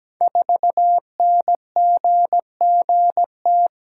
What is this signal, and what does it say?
Morse code, heard as a keyed tone, about 17 wpm, 700 Hz. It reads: 4NGGT
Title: Code Groups - 5 Letter:Numbers - 17wpm